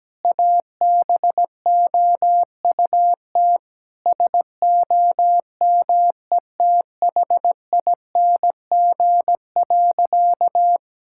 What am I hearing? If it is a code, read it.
Morse code, heard as a keyed tone, about 17 wpm, 700 Hz. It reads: ABOUT SOMETHING.
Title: Sentences from Top 1000 Words - Review Part 5 - 17wpm